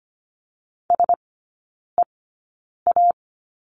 Morse code, heard as a keyed tone, about 25 wpm, spaced as for 9 wpm, 700 Hz. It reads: SEA
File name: course / 25:10wpm / 093 Lesson 16e - Letter c - Three Characters 25wpm